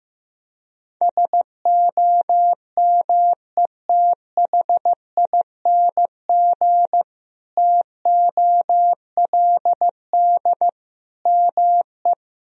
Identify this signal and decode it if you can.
Morse code, heard as a keyed tone, about 15 wpm, 700 Hz. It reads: SOMETHING TOLD ME
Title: Sentences from Top 1000 Words - Review Part 1 - 15wpm